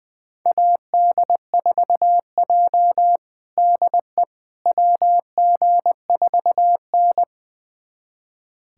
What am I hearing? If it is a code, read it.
Morse code, heard as a keyed tone, about 20 wpm, 700 Hz. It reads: AD4J DE WG4N
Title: USA Callsign DE Callsign - 4 Characters - 20wpm